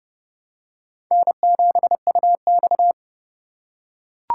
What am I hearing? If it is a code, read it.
Morse code, heard as a keyed tone, about 30 wpm, 700 Hz. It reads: N7UX
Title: USA Call Signs - 4 Characters - 2x Character Spacing - 30wpm